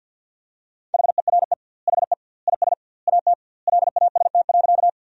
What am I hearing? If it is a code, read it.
Morse code, heard as a keyed tone, about 50 wpm, 700 Hz. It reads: HERE HE IS AT LAST.
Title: 2x Word-Spacing - Sentences from Top 300 Words - 50wpm